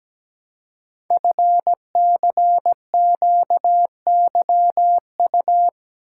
Morse code, heard as a keyed tone, about 17 wpm, 700 Hz. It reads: FCQYU
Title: Code Groups - 5 Letters - 17wpm